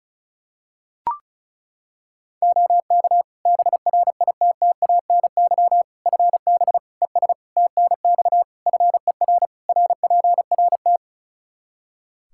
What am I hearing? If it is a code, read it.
Morse code, heard as a keyed tone, about 36 wpm, 700 Hz. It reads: OK BRITTANY FB ES TNX FER RPRT
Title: Practice QSO - Exchange 2 - Info - 35wpm